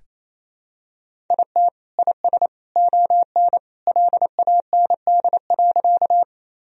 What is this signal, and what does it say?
Morse code, heard as a keyed tone, about 28 wpm, 700 Hz. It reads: IT IS ON LAND.
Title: Sentences from Top 1000 Words - Review Part 1 - 28wpm